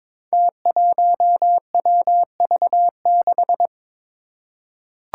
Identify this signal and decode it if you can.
Morse code, heard as a keyed tone, about 22 wpm, 700 Hz. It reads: T1WV6
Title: Five Challenging Letter-Number Combinations 22wpm